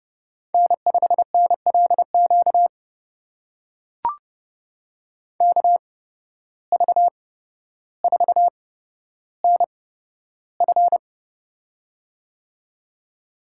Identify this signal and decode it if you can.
Morse code, heard as a keyed tone, about 30 wpm, spaced as for 13 wpm, 700 Hz. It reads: N5NLQ KV4NF
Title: USA Call Signs - 5 Characters - 8x Character Spacing - 30wpm